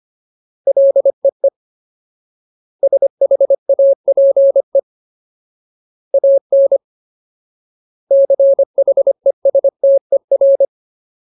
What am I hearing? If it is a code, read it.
Morse code, heard as a keyed tone, about 25 wpm, 550 Hz. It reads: LEE SHAPE AN CHESTER
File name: Ultimate Word List / 25WPM Ultimate